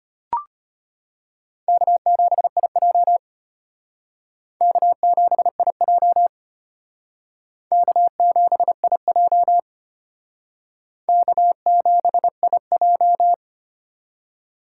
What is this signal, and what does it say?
Morse code, heard as a keyed tone, about 32 wpm, 700 Hz. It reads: K7IJ K7IJ K7IJ K7IJ